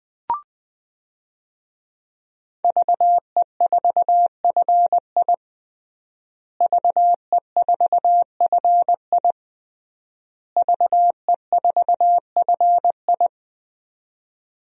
Morse code, heard as a keyed tone, about 21 wpm, 700 Hz. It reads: VE4FI VE4FI VE4FI
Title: DXCC-100 Call Signs - 5 Characters - Repeated 3x - 20wpm